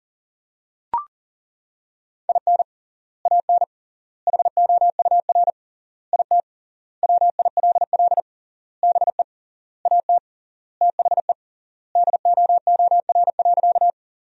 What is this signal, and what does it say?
Morse code, heard as a keyed tone, about 40 wpm, 700 Hz. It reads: IN AN HOUR IT WILL BE AT THE DOOR.
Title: 3x Word-Spacing - Sentences from Top 400 Words - 40wpm